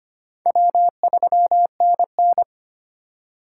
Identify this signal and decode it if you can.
Morse code, heard as a keyed tone, about 31 wpm, 700 Hz. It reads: W3NN
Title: Speed Racing - USA 1x2 Call Signs 25wpm